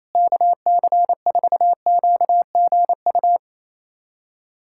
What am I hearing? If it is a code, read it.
Morse code, heard as a keyed tone, about 28 wpm, 700 Hz. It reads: KC4QGU
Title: USA Callsign DE Callsign - 6 Characters - 28wpm